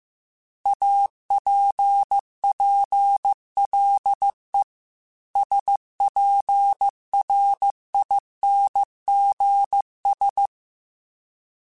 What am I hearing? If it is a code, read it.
Morse code, heard as a keyed tone, about 15 wpm, 800 Hz. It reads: APPLE SPRINGS
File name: fromSMs / FromSMs mp3s / Cities@13